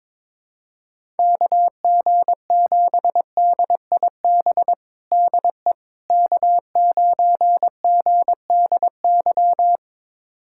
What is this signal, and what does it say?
Morse code, heard as a keyed tone, about 21 wpm, 700 Hz. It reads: KG7DIB DE K9GDY
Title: USA Callsign DE Callsign 22wpm